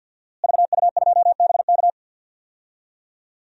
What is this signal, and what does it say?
Morse code, heard as a keyed tone, about 52 wpm, 700 Hz. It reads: VU2BK
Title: DXCC-100 Call Signs - 5 Characters - Repeated 3x - 50wpm